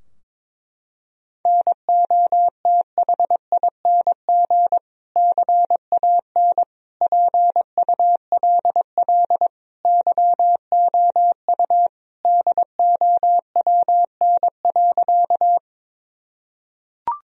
Morse code, heard as a keyed tone, about 22 wpm, 700 Hz. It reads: NOTHING CAN PULL YOU DOWN.
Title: Sentences from Top 1000 Words - Review Part 2 - 22wpm